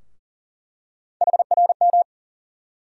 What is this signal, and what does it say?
Morse code, heard as a keyed tone, about 40 wpm, 700 Hz. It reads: HRM
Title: USA Call Sign Postfixes - 3 Letters 40wpm